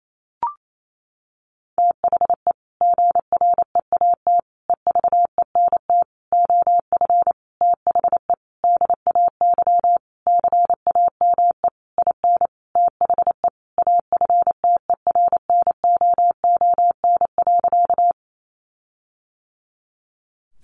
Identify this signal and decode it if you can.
Morse code, heard as a keyed tone, about 28 wpm, 700 Hz. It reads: THE GREAT EVENT OF THE DAY CAME IN THE AFTERNOON.
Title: Sentences from Top 1500 Words 28wpm